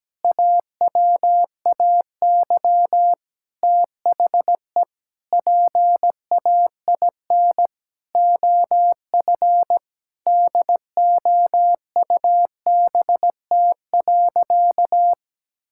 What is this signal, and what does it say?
Morse code, heard as a keyed tone, about 17 wpm, 700 Hz. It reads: AWAY THE PAIN OF DOUBT.
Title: Sentences from Top 1600 Words 17wpm